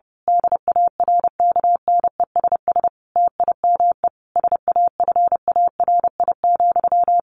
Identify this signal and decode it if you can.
Morse code, heard as a keyed tone, about 28 wpm, 700 Hz. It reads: DARKNESS TIME SAFARI,
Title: A Sound Of Thunder 30wpm